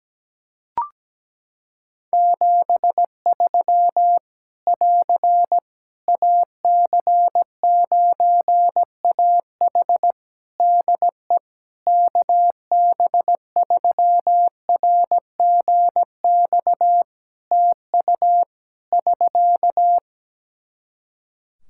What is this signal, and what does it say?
Morse code, heard as a keyed tone, about 17 wpm, 700 Hz. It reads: 73 <AR> AC9AH DE KB3RGX TU <SK>
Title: Practice QSO - Exchange 4 - Ending - 17wpm